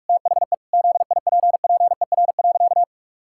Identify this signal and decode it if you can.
Morse code, heard as a keyed tone, about 43 wpm, 700 Hz. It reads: THE ZIPPER.
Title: Pangram Challenge v2 - 45wpm